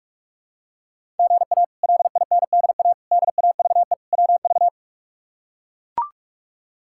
Morse code, heard as a keyed tone, about 42 wpm, 700 Hz. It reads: GA LINDA DAVE WV
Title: K1USN SST QSO Practice - 45wpm